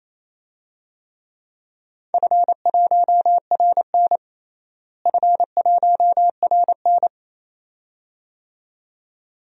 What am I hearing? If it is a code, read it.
Morse code, heard as a keyed tone, about 27 wpm, 700 Hz. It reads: F1RN F1RN
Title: DXCC-100 Call Signs - 4 Characters - Repeated 2x - 28wpm